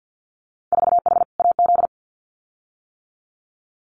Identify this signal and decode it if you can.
Morse code, heard as a keyed tone, about 50 wpm, 700 Hz. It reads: 4H AZ
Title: Winter Field Day Exchanges Encore 50wpm